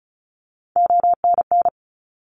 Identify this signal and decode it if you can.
Morse code, heard as a keyed tone, about 36 wpm, 700 Hz. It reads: ONN
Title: ARRL Sections 35wpm